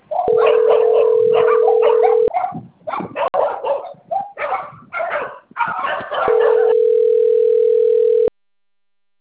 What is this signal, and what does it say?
Ringback tone (North America), 440 + 480 Hz, with noise.
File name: rg-1600-unknown- - - .2964